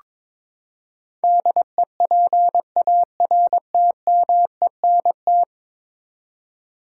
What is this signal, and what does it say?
Morse code, heard as a keyed tone, about 22 wpm, 700 Hz. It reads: DEPARTMENT
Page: morse-code-ninja-practice-22wpm